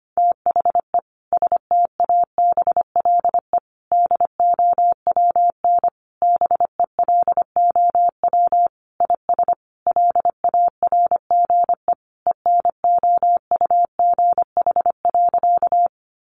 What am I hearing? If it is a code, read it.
Morse code, heard as a keyed tone, about 25 wpm, 700 Hz. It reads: THE STABLE DOWN BELOW IS LARGE ENOUGH.
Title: Sentences from Top 1500 Words 25wpm